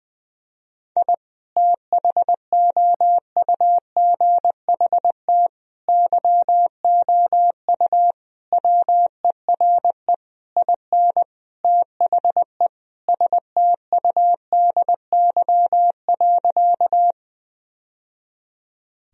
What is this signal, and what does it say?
Morse code, heard as a keyed tone, about 21 wpm, 700 Hz. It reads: I THOUGHT YOU WERE IN THE STUDY.
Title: Sentences from Top 1000 Words - Review Part 1 - 20wpm